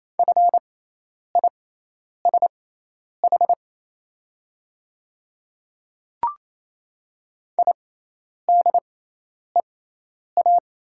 Morse code, heard as a keyed tone, about 28 wpm, spaced as for 10 wpm, 690 Hz. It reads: FISH IDEA
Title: 6x Character Spacing - Top 500 Words - 28wpm